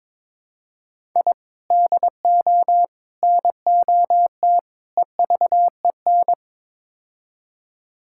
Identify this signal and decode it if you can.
Morse code, heard as a keyed tone, about 21 wpm, 700 Hz. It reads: I DO NOT EVEN
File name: practice / 22wpm / Build A Phrase - 5-Word Phrases - 22wpm